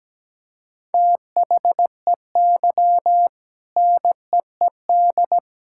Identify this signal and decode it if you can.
Morse code, heard as a keyed tone, about 17 wpm, 700 Hz. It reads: THEY NEED
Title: Sentences from Top 1000 Words - Review Part 6 - 17wpm